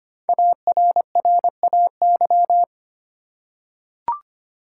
Morse code, heard as a keyed tone, about 25 wpm, 700 Hz. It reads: ARRAY
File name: 5 Letter Words - First 3 Letters Given - 25wpm